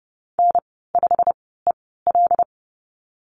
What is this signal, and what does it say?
Morse code, heard as a keyed tone, about 30 wpm, spaced as for 17 wpm, 700 Hz. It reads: N5EL
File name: Lesson 15g - Number 5 - Call Signs 30wpm